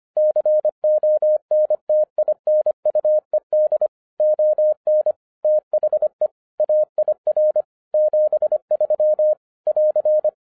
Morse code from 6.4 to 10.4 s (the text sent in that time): AIR 73 <AR>